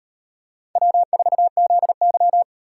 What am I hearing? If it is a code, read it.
Morse code, heard as a keyed tone, about 38 wpm, 700 Hz. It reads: W4ZY